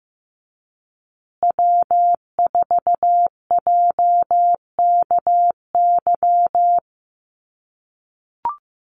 Morse code, heard as a keyed tone, about 15 wpm, 700 Hz. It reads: W4JKY
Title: USA Call Signs Repeated 2x 15wpm